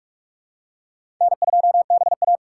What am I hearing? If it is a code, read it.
Morse code, heard as a keyed tone, about 41 wpm, 700 Hz. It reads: N2BA
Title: USA Call Signs - 4 Characters - 3x Character Spacing - 45wpm